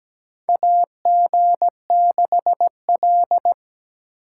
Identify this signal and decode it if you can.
Morse code, heard as a keyed tone, about 17 wpm, 700 Hz. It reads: AG6L
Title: USA Call Signs - 8x Character Spacing - 17wpm